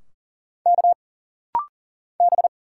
Morse code, heard as a keyed tone, about 40 wpm, 700 Hz. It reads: K B
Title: Single Letters - Rapid-Fire 40wpm